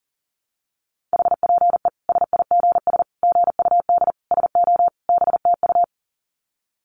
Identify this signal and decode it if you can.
Morse code, heard as a keyed tone, about 40 wpm, 700 Hz. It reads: HPE SIGS GUD SO BTU
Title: Common QSO Phrases 40wpm